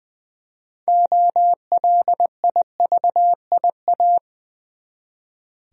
Morse code, heard as a keyed tone, about 20 wpm, 700 Hz. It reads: OLIVIA
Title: USA Common Names - 20wpm